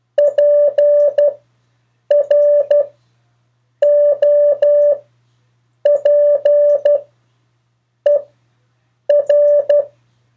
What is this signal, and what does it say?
Morse code, heard as a keyed tone, about 12 wpm, spaced as for 7 wpm, 580 Hz. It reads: PROPER
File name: 4of4-sentences-mistakes-test